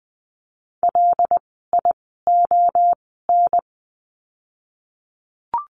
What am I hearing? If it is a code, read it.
Morse code, heard as a keyed tone, about 20 wpm, spaced as for 15 wpm, 700 Hz. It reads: LION
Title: 075 Lesson 13f - Letter L - Words 20wpm